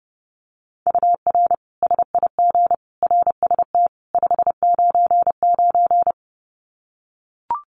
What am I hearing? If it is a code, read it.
Morse code, heard as a keyed tone, about 30 wpm, 700 Hz. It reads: UR SIG RST 599
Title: Common QSO Phrases 30wpm